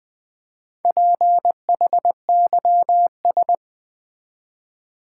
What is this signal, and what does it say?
Morse code, heard as a keyed tone, about 20 wpm, 700 Hz. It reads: PHYS
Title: Build A Word - 2 Letters At A Time - 8-Letter Words - 20wpm